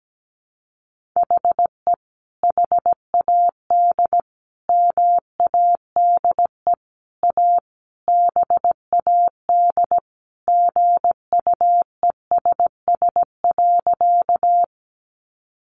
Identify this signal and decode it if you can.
Morse code, heard as a keyed tone, about 17 wpm, 700 Hz. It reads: HE HAD MADE A BAD GUESS.
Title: Sentences from Top 1500 Words 17wpm